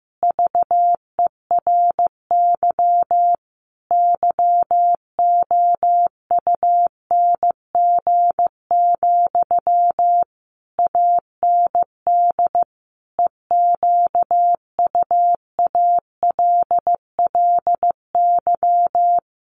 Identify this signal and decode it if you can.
Morse code, heard as a keyed tone, about 15 wpm, 700 Hz. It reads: VERY YOUNG, AND EQUALLY
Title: Sentences from Top 1400 Words 15wpm